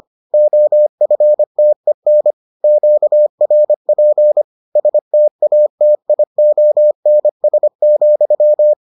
Morse code, heard as a keyed tone, about 25 wpm, 600 Hz. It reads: OFTEN QRP STATIONS,